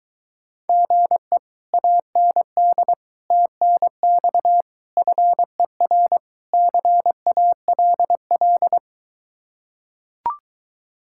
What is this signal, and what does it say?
Morse code, heard as a keyed tone, about 23 wpm, 700 Hz. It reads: GE AND TNX FER CALL